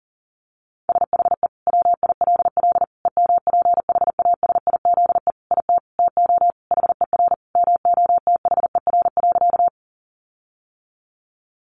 Morse code, heard as a keyed tone, about 37 wpm, 700 Hz. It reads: SHE WILL EMPHASIZE IT TO HER MOTHER.
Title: Sentences from Top 1500 Words 40wpm